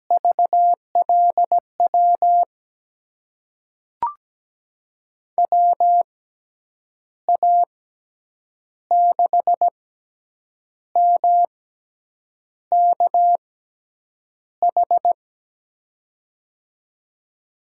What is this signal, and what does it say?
Morse code, heard as a keyed tone, about 17 wpm, spaced as for 7 wpm, 700 Hz. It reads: VLW WA6MKH
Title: USA Call Signs - 6 Characters - 6x Character Spacing - 17wpm